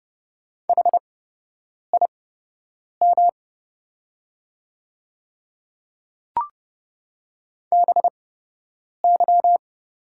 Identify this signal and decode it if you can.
Morse code, heard as a keyed tone, about 30 wpm, spaced as for 8 wpm, 690 Hz. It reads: HIM BY
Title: 8x Character Spacing - Top 200 Words - 30wpm